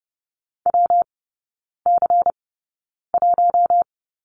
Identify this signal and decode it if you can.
Morse code, heard as a keyed tone, about 30 wpm, spaced as for 9 wpm, 700 Hz. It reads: WC1
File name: Lesson 19e - Letter w - Three Characters 30wpm